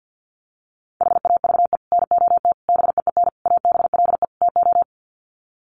Morse code, heard as a keyed tone, about 50 wpm, 700 Hz. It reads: HAVE NOT BEEN ABLE TO
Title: Sets of 5 Words 50wpm